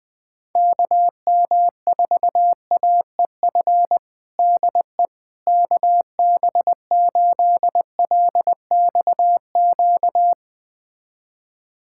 Morse code, heard as a keyed tone, about 20 wpm, 700 Hz. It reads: KM4AEF DE KB8LXQ
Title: USA Callsign DE Callsign - 6 Characters - 20wpm